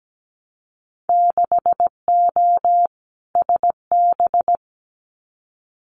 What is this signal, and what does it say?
Morse code, heard as a keyed tone, about 17 wpm, 700 Hz. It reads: 6O SB
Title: Winter Field Day Exchanges Encore 17wpm